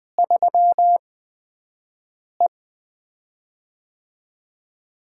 Morse code, heard as a keyed tone, about 20 wpm, 700 Hz. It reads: 3 E
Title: USA Call Signs - 4 Characters - 8x Character Spacing - 20wpm